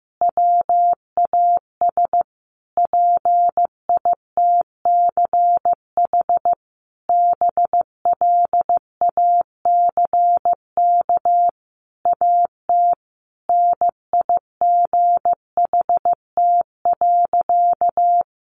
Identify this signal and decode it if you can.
Morse code, heard as a keyed tone, about 15 wpm, 700 Hz. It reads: WAS PITCH BLACK AT NIGHT.
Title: Sentences from Top 1300 Words 15wpm